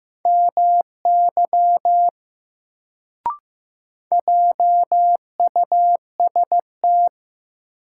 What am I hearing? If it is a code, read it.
Morse code, heard as a keyed tone, about 15 wpm, 700 Hz. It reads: MY JUST
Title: Sentences from Top 200 Words - New Word Drill - 15wpm